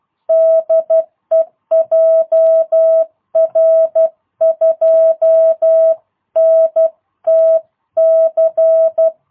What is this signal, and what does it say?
Morse code, heard as a keyed tone, about 12 wpm, 650 Hz. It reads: DE JR2NTC
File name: jh8cbh / 2022cw